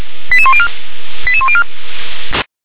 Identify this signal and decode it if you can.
Five-tone selective call (ZVEI), heard as quick runs of five tones, tones of about 70 ms. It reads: 88104 88104
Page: /357/SDR/ZVEI/88020/2024.04.15/